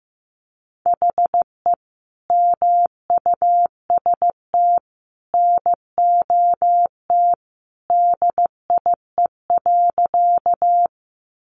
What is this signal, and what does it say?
Morse code, heard as a keyed tone, about 15 wpm, 700 Hz. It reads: HE MUST NOT DIE.
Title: Sentences from Top 1300 Words 15wpm